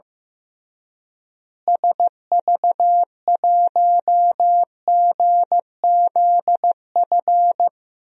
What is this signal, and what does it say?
Morse code, heard as a keyed tone, about 15 wpm, 700 Hz. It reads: SV1GZF
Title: DXCC-100 Call Signs - 6 Characters - Repeated 3x - 15wpm